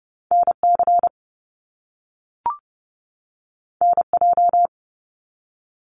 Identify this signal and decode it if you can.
Morse code, heard as a keyed tone, about 30 wpm, 700 Hz. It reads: NC NJ
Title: US States 30wpm